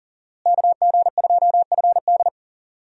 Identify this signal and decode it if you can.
Morse code, heard as a keyed tone, about 40 wpm, 700 Hz. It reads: KG2FD
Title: USA Call Signs - 5 Characters - Repeated 2x - 40wpm